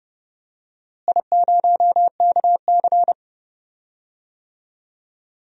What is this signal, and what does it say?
Morse code, heard as a keyed tone, about 32 wpm, 700 Hz. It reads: I0KC
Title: DXCC-100 Call Signs - 4 Characters - Repeated 2x - 30wpm